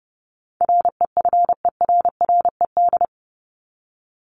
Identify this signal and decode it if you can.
Morse code, heard as a keyed tone, about 30 wpm, 700 Hz. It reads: REFERRED
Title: ED Suffix - Top 100 Words with Base Words 30wpm